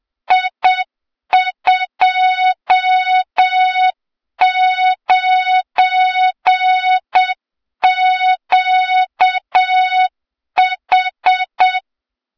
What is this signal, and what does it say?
Morse code, heard as a keyed tone, about 7 wpm, 3700 Hz, with noise. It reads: I29QH
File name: Morze-HA2MN / hang / 23 GYAKORLAT